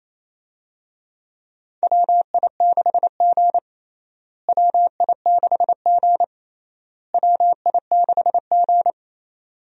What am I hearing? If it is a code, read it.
Morse code, heard as a keyed tone, about 28 wpm, 700 Hz. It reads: WI6G WI6G WI6G
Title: USA Call Signs - 4 Characters - Repeated 3x - 28wpm